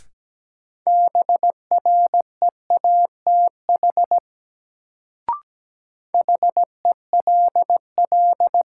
Morse code, heard as a keyed tone, about 17 wpm, 700 Hz. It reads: BREATH HELL
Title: Sets of 5 Words Encore - New Word Drill - 17wpm